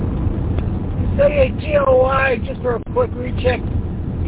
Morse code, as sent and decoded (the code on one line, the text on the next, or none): none